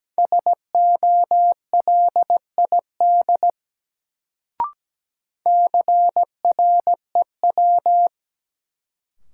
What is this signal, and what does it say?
Morse code, heard as a keyed tone, about 17 wpm, 700 Hz. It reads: SOLID CREW
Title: Sentences from Top 1500 Words - New Word Drill - 17wpm